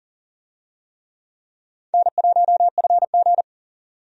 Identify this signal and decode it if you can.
Morse code, heard as a keyed tone, about 38 wpm, 700 Hz. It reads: N1FG
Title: USA Call Signs - 4 Characters - 40wpm